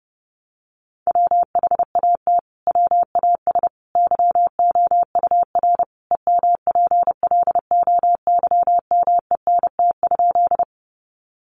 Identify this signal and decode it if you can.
Morse code, heard as a keyed tone, about 30 wpm, 700 Hz. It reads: WHAT WAS YOUR EMPLOYMENT?
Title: Sentences from Top 1500 Words 30wpm